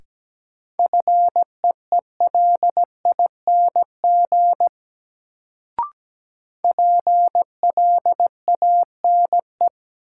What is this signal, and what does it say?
Morse code, heard as a keyed tone, about 17 wpm, 700 Hz. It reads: FEELING PLANE
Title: Sentences from Top 1000 Words - Review Part 3 - New Word Drill - 17wpm